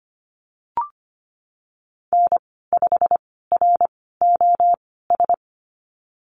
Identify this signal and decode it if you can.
Morse code, heard as a keyed tone, about 25 wpm, spaced as for 16 wpm, 700 Hz. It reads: N5ROS
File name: Lesson 15g - Number 5 - Call Signs 25wpm